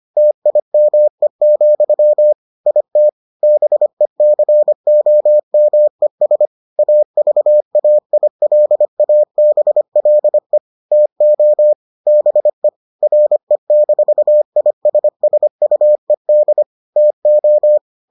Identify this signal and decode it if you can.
Morse code, heard as a keyed tone, about 25 wpm, 600 Hz. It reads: TIME, IT BECOMES AVAILABLE TO BE RE-ISSUED TO